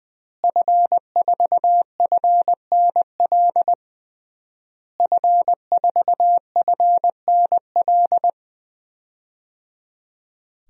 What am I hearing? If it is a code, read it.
Morse code, heard as a keyed tone, about 20 wpm, 700 Hz. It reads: F4FNL F4FNL
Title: DXCC-100 Call Signs - 5 Characters - Repeated 3x - 20wpm